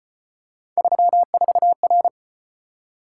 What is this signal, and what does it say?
Morse code, heard as a keyed tone, about 34 wpm, 700 Hz. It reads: 34R